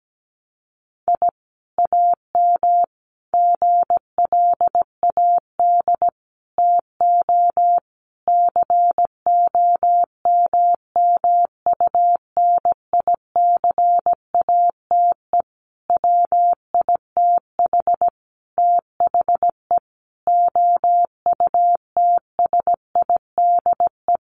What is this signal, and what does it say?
Morse code, heard as a keyed tone, about 17 wpm, 700 Hz. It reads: I AM GLAD TO COMMUNICATE WITH THE OUTSIDE
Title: Sentences from Top 1300 Words 17wpm